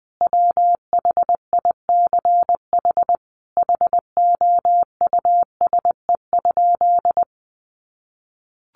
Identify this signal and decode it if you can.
Morse code, heard as a keyed tone, about 20 wpm, 700 Hz. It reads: WHICH HOUSE?
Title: Sentences from Top 1100 Words 20wpm